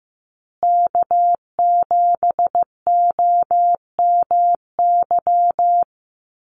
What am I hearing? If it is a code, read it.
Morse code, heard as a keyed tone, about 15 wpm, 700 Hz. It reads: K7OMY
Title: USA Call Signs Repeated 3x 15wpm